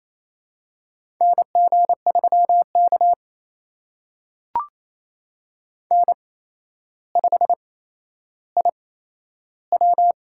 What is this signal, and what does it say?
Morse code, heard as a keyed tone, about 28 wpm, spaced as for 12 wpm, 700 Hz. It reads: NG3K N5IW
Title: USA Call Signs - 4 Characters - 8x Character Spacing - 28wpm